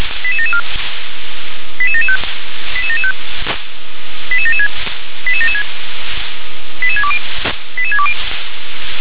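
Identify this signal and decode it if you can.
Five-tone selective call (ZVEI), heard as quick runs of five tones, tones of about 70 ms. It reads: 88774 88775 88775 88776 88776 88620 88620
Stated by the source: SDR-received ZVEI five-tone selective call.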